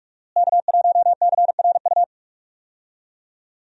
Morse code, heard as a keyed tone, about 46 wpm, 700 Hz. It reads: K1CRU
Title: USA Call Signs - 5 Characters - Repeated 3x - 45wpm